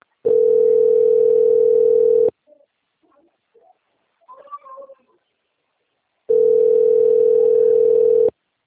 Ringback tone (North America), 440 + 480 Hz.
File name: out- - - - - .40324